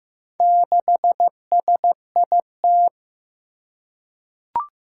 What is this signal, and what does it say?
Morse code, heard as a keyed tone, about 15 wpm, 700 Hz. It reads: 6SIT
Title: Four Challenging Letter-Number Combinations 15wpm